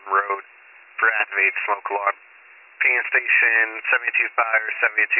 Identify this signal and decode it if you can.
Emergency Alert System (SAME) tones, heard as harsh buzzing bursts, with speech. It is heard from 0.9 to 4.9 s.